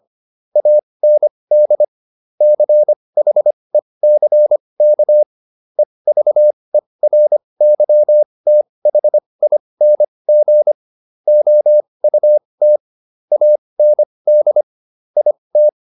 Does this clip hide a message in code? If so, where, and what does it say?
Yes: Morse code from 0.6 to 15.7 s, heard as a keyed tone, about 25 wpm, spaced as for 20 wpm, 600 Hz. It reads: AND CHECK EVERYTHING OUT AND IT